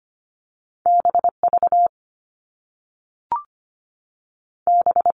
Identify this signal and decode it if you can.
Morse code, heard as a keyed tone, about 25 wpm, 700 Hz. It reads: BV B
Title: All Cut Numbers - 2-Digit 25wpm